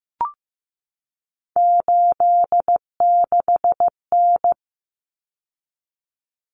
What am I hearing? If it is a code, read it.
Morse code, heard as a keyed tone, about 15 wpm, 700 Hz. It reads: 86N
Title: Common Cut Numbers - 3-Digit 15wpm